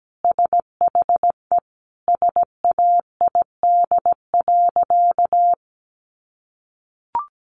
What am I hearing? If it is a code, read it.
Morse code, heard as a keyed tone, about 17 wpm, 700 Hz. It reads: SHE SAID.
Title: Sentences from Top 1000 Words 17wpm